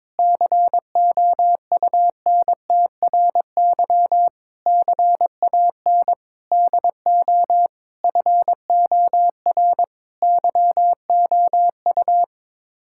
Morse code, heard as a keyed tone, about 21 wpm, 700 Hz. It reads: COUNTRY CAN DO FOR YOU
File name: Sets of 9 Words - 22wpm